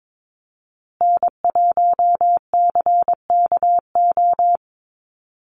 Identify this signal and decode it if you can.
Morse code, heard as a keyed tone, about 22 wpm, 700 Hz. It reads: N1CKO
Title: USA Call Signs Repeated 3x 22wpm